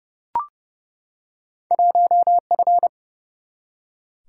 Morse code, heard as a keyed tone, about 31 wpm, 700 Hz. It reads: 1F
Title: Single-Letter Single-Number 30wpm